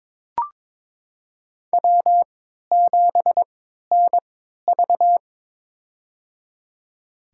Morse code, heard as a keyed tone, about 22 wpm, spaced as for 13 wpm, 700 Hz. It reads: W7NV
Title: USA Call Signs - 4 Characters - 3x Character Spacing - 22wpm